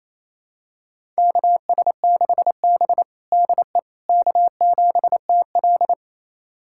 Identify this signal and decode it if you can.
Morse code, heard as a keyed tone, about 28 wpm, 700 Hz. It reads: KS6B DE K7TL
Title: USA Callsign DE Callsign - 4 Characters - 28wpm